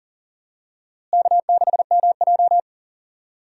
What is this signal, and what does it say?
Morse code, heard as a keyed tone, about 40 wpm, 700 Hz. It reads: K6MJ
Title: USA Call Signs - 4 Characters - Repeated 3x - 40wpm